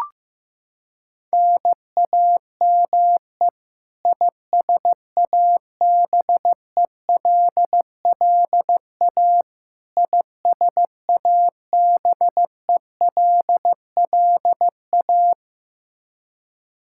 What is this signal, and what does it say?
Morse code, heard as a keyed tone, about 15 wpm, 700 Hz. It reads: NAME ISABELLA ISABELLA
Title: Practice QSO - Exchange 1 - Beginning - 15wpm